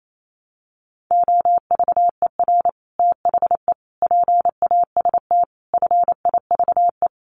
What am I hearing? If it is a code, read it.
Morse code, heard as a keyed tone, about 28 wpm, 700 Hz. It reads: OVER THE PAST FIVE
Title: Sets of 5 Words Encore 28wpm